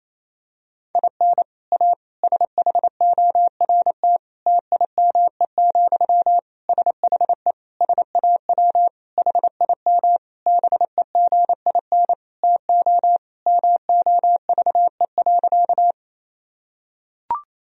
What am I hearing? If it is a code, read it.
Morse code, heard as a keyed tone, about 28 wpm, 700 Hz. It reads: IN A SHORT TIME, SHE SAW HIM BEGIN TO MOVE.
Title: Sentences from Top 1000 Words - Review Part 2 - 28wpm